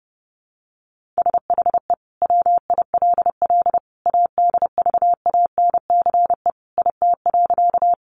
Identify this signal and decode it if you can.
Morse code, heard as a keyed tone, about 30 wpm, 700 Hz. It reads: SHE WILL ADVANCE IT.
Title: Sentences from Top 1400 Words 30wpm